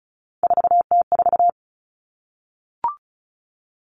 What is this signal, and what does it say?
Morse code, heard as a keyed tone, about 34 wpm, 700 Hz. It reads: 4T4
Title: Common Cut Numbers - 3-Digit 35wpm